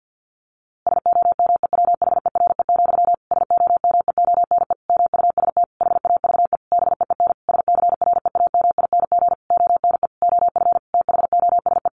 Morse code, heard as a keyed tone, about 50 wpm, 700 Hz. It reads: SOMEWHERE, SOMEONE MUST HAVE BEEN SCREAMING ONE OF THOSE
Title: A Sound Of Thunder 50wpm